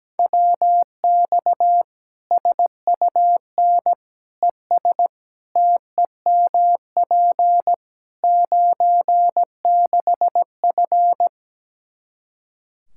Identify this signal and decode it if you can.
Morse code, heard as a keyed tone, about 17 wpm, 700 Hz. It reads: WX SUN ES TEMP 96F
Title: Practice QSO - Exchange 2 - Info - 17wpm